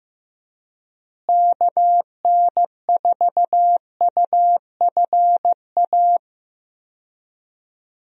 Morse code, heard as a keyed tone, about 15 wpm, 700 Hz. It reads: KN4UFA
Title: USA Call Signs - 6 Characters - Repeated 3x - 15wpm